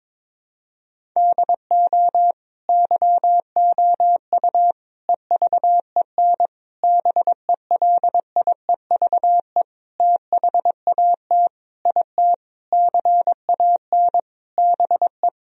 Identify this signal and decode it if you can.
Morse code, heard as a keyed tone, about 22 wpm, 700 Hz. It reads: DO YOU EVEN BELIEVE THAT IT CAN BE
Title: Sentences from Top 1000 Words - Review Part 1 - 22wpm